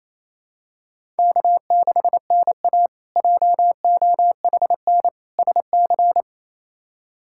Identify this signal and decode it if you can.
Morse code, heard as a keyed tone, about 28 wpm, 700 Hz. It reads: K6NA JOHN SC
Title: K1USN SST QSO Practice - 28wpm